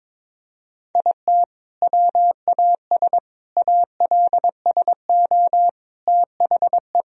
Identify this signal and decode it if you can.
Morse code, heard as a keyed tone, about 22 wpm, 700 Hz. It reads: IT WAS ALSO THE
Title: Sentences from Top 1000 Words - Review Part 2 - 22wpm